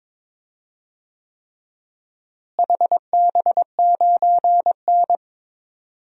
Morse code, heard as a keyed tone, about 22 wpm, 700 Hz. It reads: HB9N
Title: DXCC-100 Call Signs - 4 Characters - Repeated 3x - 22wpm